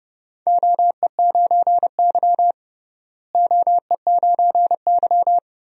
Morse code, heard as a keyed tone, about 29 wpm, 700 Hz. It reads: OE9Y OE9Y
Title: DXCC-100 Call Signs - 4 Characters - Repeated 2x - 30wpm